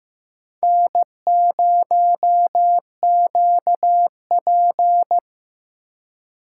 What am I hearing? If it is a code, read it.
Morse code, heard as a keyed tone, about 15 wpm, 700 Hz. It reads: N0QP
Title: USA Call Signs - 4 Characters - 4x Character Spacing - 15wpm